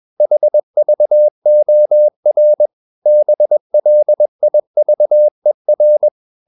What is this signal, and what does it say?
Morse code, heard as a keyed tone, about 21 wpm, 600 Hz. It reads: HVOR BLIVER
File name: Ferie QRS 21wpm